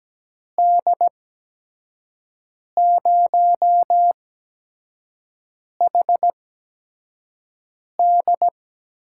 Morse code, heard as a keyed tone, about 17 wpm, spaced as for 5 wpm, 700 Hz. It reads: D0HD
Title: USA Call Signs - 8x Character Spacing - 17wpm